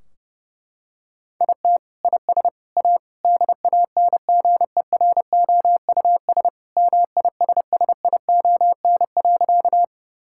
Morse code, heard as a keyed tone, about 30 wpm, 700 Hz. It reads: IT IS A DANGEROUS MISSION.
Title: Sentences from Top 1600 Words 30wpm